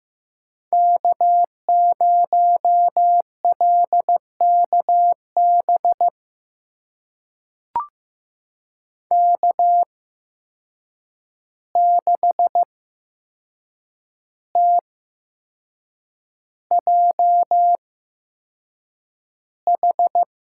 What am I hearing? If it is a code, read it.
Morse code, heard as a keyed tone, about 15 wpm, spaced as for 6 wpm, 700 Hz. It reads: K0LKB K6TJH
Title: USA Call Signs - 8x Character Spacing - 15wpm